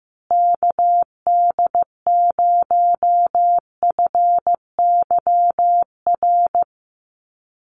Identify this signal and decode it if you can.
Morse code, heard as a keyed tone, about 14 wpm, 700 Hz. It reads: KD0FYR
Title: USA Call Signs Repeated 2x 15wpm